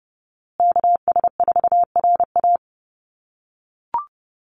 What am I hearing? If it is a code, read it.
Morse code, heard as a keyed tone, about 30 wpm, 700 Hz. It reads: KS4RA
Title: USA Call Signs Repeated 2x 30wpm